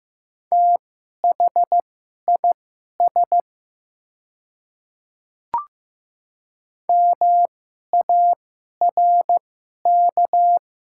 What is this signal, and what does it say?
Morse code, heard as a keyed tone, about 15 wpm, spaced as for 11 wpm, 700 Hz. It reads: THIS MARK
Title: 2x Character Spacing - Top 500 Words - 15wpm